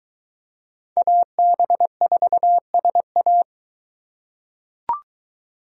Morse code, heard as a keyed tone, about 23 wpm, 700 Hz. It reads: AB4SA